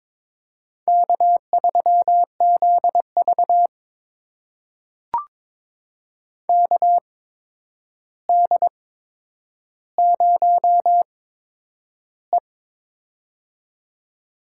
Morse code, heard as a keyed tone, about 22 wpm, spaced as for 9 wpm, 700 Hz. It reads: K3ZV KD0E
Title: USA Call Signs - 4 Characters - 8x Character Spacing - 22wpm